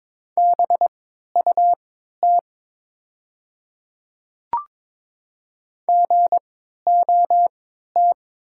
Morse code, heard as a keyed tone, about 22 wpm, spaced as for 12 wpm, 700 Hz. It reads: BUT GOT
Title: 3x Character Spacing - Top 300 Words - 22wpm